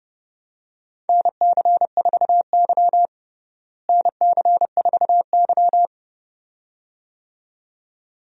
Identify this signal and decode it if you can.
Morse code, heard as a keyed tone, about 32 wpm, 700 Hz. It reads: NC4Y NC4Y
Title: USA Call Signs - 4 Characters - Repeated 2x - 30wpm